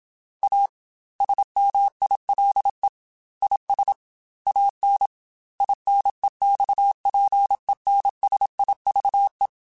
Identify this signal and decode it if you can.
Morse code, heard as a keyed tone, about 27 wpm, 800 Hz. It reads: A SMILE IS AN INEXPENSIVE